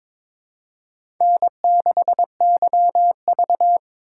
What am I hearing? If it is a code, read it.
Morse code, heard as a keyed tone, about 22 wpm, 700 Hz. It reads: N6YV
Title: USA Call Signs - 4 Characters - 2x Character Spacing - 22wpm